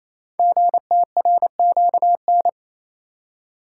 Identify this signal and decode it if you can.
Morse code, heard as a keyed tone, about 27 wpm, 700 Hz. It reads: GTRQN
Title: Code Groups - 5 Letters - 28wpm